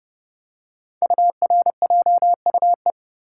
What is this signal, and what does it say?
Morse code, heard as a keyed tone, about 30 wpm, 700 Hz. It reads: URJUE
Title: Code Groups - 5 Letters - 30wpm